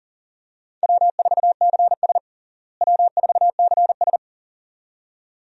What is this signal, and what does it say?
Morse code, heard as a keyed tone, about 40 wpm, 700 Hz. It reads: W4CS W4CS
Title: USA Call Signs - 4 Characters - Repeated 2x - 40wpm